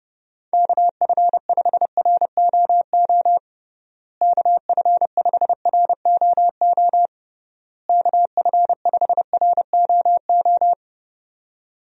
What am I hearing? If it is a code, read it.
Morse code, heard as a keyed tone, about 32 wpm, 700 Hz. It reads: KF5ROO KF5ROO KF5ROO
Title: USA Call Signs - 6 Characters - Repeated 3x - 30wpm